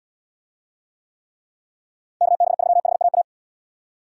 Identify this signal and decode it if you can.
Morse code, heard as a keyed tone, about 78 wpm, 700 Hz. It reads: KB3HAA